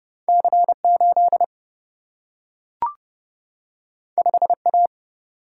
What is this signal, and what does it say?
Morse code, heard as a keyed tone, about 29 wpm, 700 Hz. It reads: C8 5A
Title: Single-Letter Single-Number 30wpm